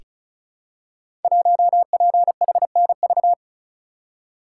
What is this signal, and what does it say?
Morse code, heard as a keyed tone, about 36 wpm, 700 Hz. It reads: 1PHNV
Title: Code Groups - 5 Letter:Numbers - 35wpm